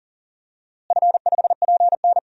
Morse code, heard as a keyed tone, about 40 wpm, 700 Hz. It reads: F5PN
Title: DXCC-100 Call Signs - 4 Characters - Repeated 2x - 40wpm